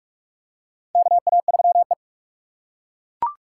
Morse code, heard as a keyed tone, about 48 wpm, 690 Hz. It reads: KA3E
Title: USA Call Signs - 4 Characters - 3x Character Spacing - 45wpm